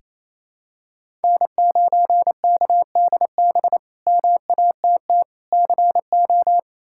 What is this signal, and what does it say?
Morse code, heard as a keyed tone, about 28 wpm, 700 Hz. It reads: N9KDB MATT CO
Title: K1USN SST QSO Practice - 28wpm